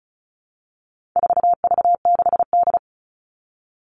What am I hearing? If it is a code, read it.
Morse code, heard as a keyed tone, about 34 wpm, 700 Hz. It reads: 4V6D
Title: All Cut Numbers - 4-Digit 35wpm